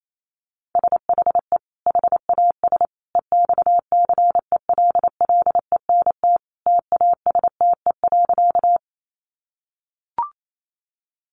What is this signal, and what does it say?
Morse code, heard as a keyed tone, about 28 wpm, 700 Hz. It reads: SHE HAS EXCELLENT TASTE.
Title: Sentences from Top 1100 Words 28wpm